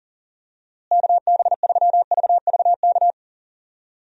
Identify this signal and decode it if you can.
Morse code, heard as a keyed tone, about 40 wpm, 700 Hz. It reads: KB3VVK
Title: USA Call Signs - 6 Characters - Repeated 3x - 40wpm